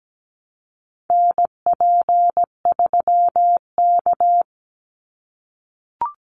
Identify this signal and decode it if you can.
Morse code, heard as a keyed tone, about 17 wpm, 700 Hz. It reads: NP3K
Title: Call Signs - 17wpm